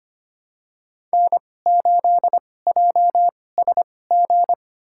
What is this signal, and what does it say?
Morse code, heard as a keyed tone, about 25 wpm, spaced as for 18 wpm, 700 Hz. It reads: N8JSG
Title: USA Call Signs - 5 Characters - 2x Character Spacing - 25wpm